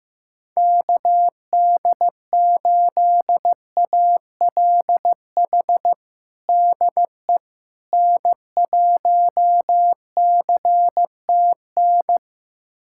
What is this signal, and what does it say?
Morse code, heard as a keyed tone, about 15 wpm, 700 Hz. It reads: KD8ALH DE N1CTN
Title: USA Callsign DE Callsign 15wpm